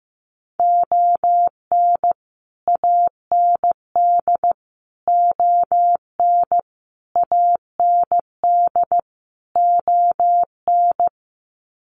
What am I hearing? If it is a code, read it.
Morse code, heard as a keyed tone, about 15 wpm, 700 Hz. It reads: ON AND ON AND ON
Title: Sets of 5 Words Encore2 15wpm